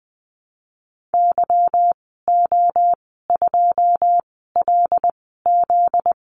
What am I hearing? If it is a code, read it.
Morse code, heard as a keyed tone, about 19 wpm, spaced as for 14 wpm, 700 Hz. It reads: YO2LZ
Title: 245 Lesson 39g - Letter z - Call Signs 20wpm